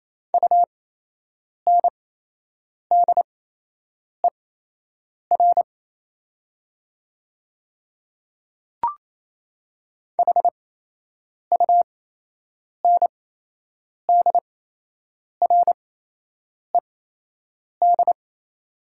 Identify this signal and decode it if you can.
Morse code, heard as a keyed tone, about 28 wpm, spaced as for 8 wpm, 700 Hz. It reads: UNDER HUNDRED
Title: 8x Character Spacing - Top 500 Words - 28wpm